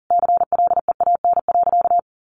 Morse code, heard as a keyed tone, about 40 wpm, 700 Hz. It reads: CLEAN.
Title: Easy Phrases 40wpm